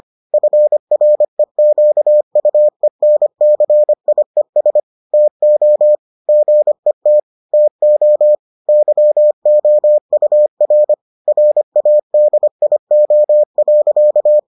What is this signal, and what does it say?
Morse code, heard as a keyed tone, about 25 wpm, 600 Hz. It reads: FREQUENCIES TO GET TO YOUR RADIO.